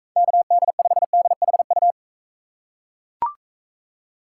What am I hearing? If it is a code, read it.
Morse code, heard as a keyed tone, about 44 wpm, 700 Hz. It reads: KD5DHU